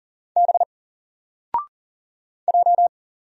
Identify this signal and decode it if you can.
Morse code, heard as a keyed tone, about 40 wpm, 700 Hz. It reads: B J
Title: Single Letter-Number 40wpm